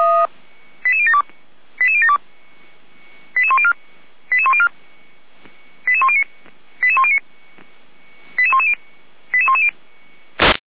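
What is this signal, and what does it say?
Five-tone selective call (ZVEI), heard as quick runs of five tones, tones of about 80 ms. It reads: 88072 88072 88104 88104 88109 88109 88110 88110
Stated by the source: ZVEI five-tone selective call received by SDR.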